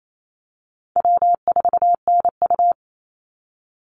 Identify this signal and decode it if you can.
Morse code, heard as a keyed tone, about 28 wpm, 700 Hz. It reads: W4NU
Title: Easy Call Signs Repeated 3x - 28wpm